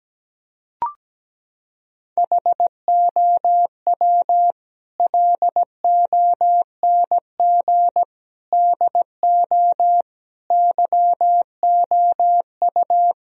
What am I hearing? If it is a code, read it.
Morse code, heard as a keyed tone, about 17 wpm, 700 Hz. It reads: HOW LONG DO YOU
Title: Sentences from Top 1000 Words - Review Part 2 - 17wpm